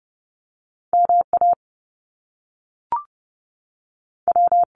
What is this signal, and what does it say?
Morse code, heard as a keyed tone, about 30 wpm, spaced as for 6 wpm, 700 Hz. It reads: MAW
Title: US States 30wpm